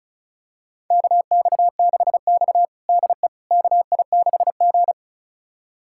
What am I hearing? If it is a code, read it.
Morse code, heard as a keyed tone, about 36 wpm, 700 Hz. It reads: KX6X DE KI6G
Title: USA Callsign DE Callsign - 4 Characters - 35wpm